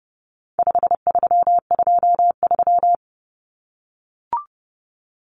Morse code, heard as a keyed tone, about 30 wpm, 700 Hz. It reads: 5323
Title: Numbers - 4-Digit 30wpm